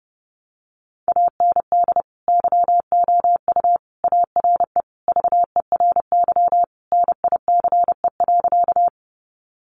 Morse code, heard as a keyed tone, about 30 wpm, 700 Hz. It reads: AND YOU ARE VERY NICE.
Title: Sentences from Top 1100 Words 30wpm